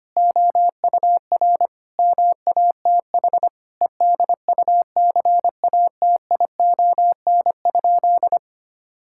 Morse code, heard as a keyed tone, about 25 wpm, 700 Hz. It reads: OUR MATH EDUCATION?
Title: Sentences from Top 1000 Words - Review Part 6 - 25wpm